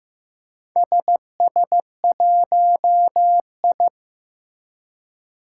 Morse code, heard as a keyed tone, about 15 wpm, 700 Hz. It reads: SS1I
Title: Four Challenging Letter-Number Combinations 15wpm